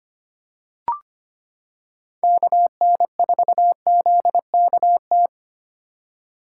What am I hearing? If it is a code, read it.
Morse code, heard as a keyed tone, about 25 wpm, 700 Hz. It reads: KN4ZKT
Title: K1USN SST QSO Practice - 25wpm